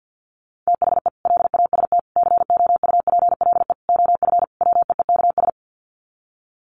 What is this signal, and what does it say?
Morse code, heard as a keyed tone, about 50 wpm, 700 Hz. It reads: THE LAST COUPLE OF WEEKS
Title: Sets of 5 Words Encore2 50wpm